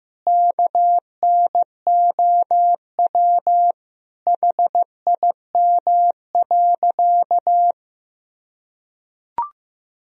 Morse code, heard as a keyed tone, about 15 wpm, 700 Hz. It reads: KNOW HIM.
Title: Sentences from Top 1000 Words - Review Part 3 - 15wpm